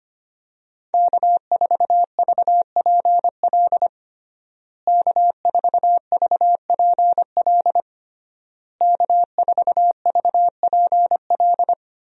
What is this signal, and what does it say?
Morse code, heard as a keyed tone, about 25 wpm, 700 Hz. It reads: K4VPL K4VPL K4VPL
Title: USA Call Signs - 5 Characters - Repeated 3x - 25wpm